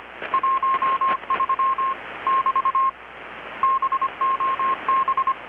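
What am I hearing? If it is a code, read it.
Morse code, heard as a keyed tone, about 25 wpm, 1070 Hz. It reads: 13 <BT> BOB
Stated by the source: Morse from KPH's Night of Nights XX maritime radiotelegraphy event, received in Chiapas, Mexico about 3,800 km away on a Kenwood R-600 with a grounded 10 m long-wire antenna.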